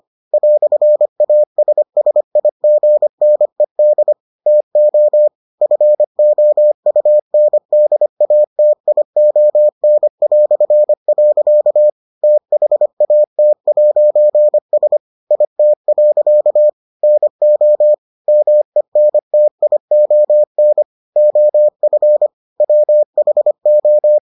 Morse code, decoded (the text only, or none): "ASSIGNED TO FOUNDATION". THAT'S IT. NO MENTION OF WHO